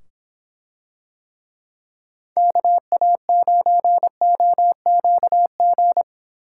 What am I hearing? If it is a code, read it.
Morse code, heard as a keyed tone, about 25 wpm, 700 Hz. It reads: KA9OQG